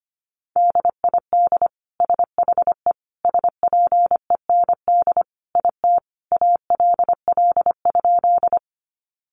Morse code, heard as a keyed tone, about 25 wpm, 700 Hz. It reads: DID SHE SPEND IT ALL?
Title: Sentences from Top 800 Words 25wpm